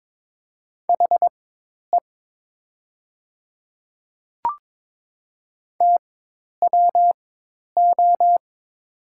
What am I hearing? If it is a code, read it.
Morse code, heard as a keyed tone, about 21 wpm, spaced as for 10 wpm, 700 Hz. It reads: HE TWO
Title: 4x Character Spacing - Top 100 Words - 22wpm